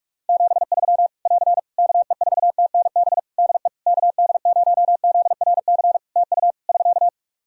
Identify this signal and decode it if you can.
Morse code, heard as a keyed tone, about 44 wpm, 700 Hz. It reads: 73 <AR> KE4TNB DE KD0ZRC TU <SK>
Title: K1USN SST QSO Practice 45wpm